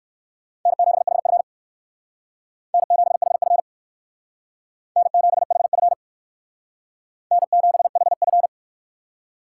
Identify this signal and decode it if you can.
Morse code, heard as a keyed tone, about 56 wpm, 700 Hz. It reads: N7HF N7HF N7HF N7HF